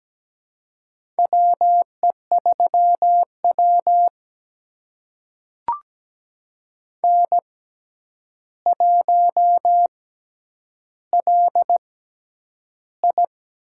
Morse code, heard as a keyed tone, about 17 wpm, spaced as for 9 wpm, 700 Hz. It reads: WE3W N1LI
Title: USA Call Signs - 4 Characters - 6x Character Spacing - 17wpm